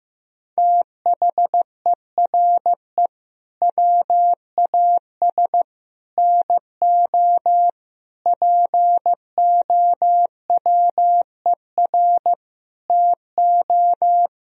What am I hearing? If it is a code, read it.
Morse code, heard as a keyed tone, about 15 wpm, 700 Hz. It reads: THERE WAS NO POWER TO
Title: Sentences from Top 1000 Words - Review Part 3 - 15wpm